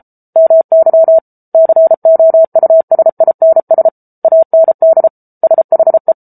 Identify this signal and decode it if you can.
Morse code, heard as a keyed tone, about 34 wpm, 650 Hz. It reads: MY COUSINS AND SHE
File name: Book Two / 07-Chapter 7 - 33 WPM